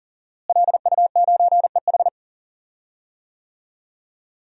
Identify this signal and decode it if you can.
Morse code, heard as a keyed tone, about 40 wpm, 700 Hz. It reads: LU9EH
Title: DXCC-100 Call Signs - 5 Characters - Repeated 3x - 40wpm